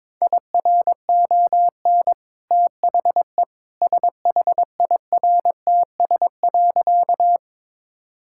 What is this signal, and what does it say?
Morse code, heard as a keyed tone, about 22 wpm, 690 Hz. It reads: IRON THE SHIRTS.
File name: practice / 22wpm / Easy Sentences with Hints - 22wpm